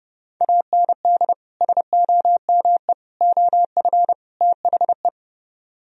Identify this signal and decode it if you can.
Morse code, heard as a keyed tone, about 31 wpm, 700 Hz. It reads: AND SOME OF THE